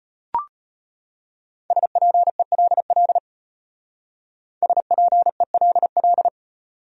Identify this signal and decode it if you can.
Morse code, heard as a keyed tone, about 30 wpm, 700 Hz. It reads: SPELL SPELL
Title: Speed-Racing Top 300 Words 25wpm